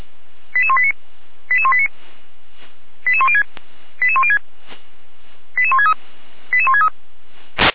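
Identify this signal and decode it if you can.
Five-tone selective call (ZVEI), heard as quick runs of five tones, tones of about 80 ms. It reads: 88189 88189 88106 88106 88163 88163
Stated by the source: Five-tone ZVEI selective call received by SDR, from feuerwehr-orlamuende.org fire brigade audio archive.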